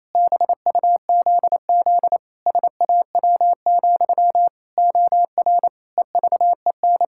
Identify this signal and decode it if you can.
Morse code, heard as a keyed tone, about 28 wpm, 700 Hz. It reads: BUZZ SAW, OR EVEN
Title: Pangram Challenge v2 - 28wpm